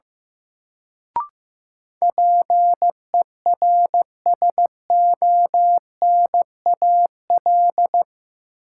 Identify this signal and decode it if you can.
Morse code, heard as a keyed tone, about 15 wpm, 700 Hz. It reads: PERSONAL
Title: Sentences from Top 700 Words - New Word Drill - 15wpm